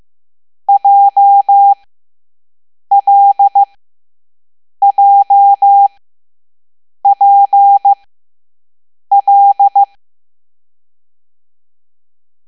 Morse code, heard as a keyed tone, about 15 wpm, spaced as for 6 wpm, 800 Hz. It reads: JLJPL